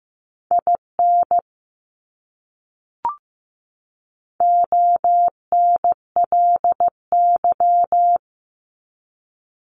Morse code, heard as a keyed tone, about 15 wpm, 700 Hz. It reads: IN ONLY
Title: Top 200 Words 15wpm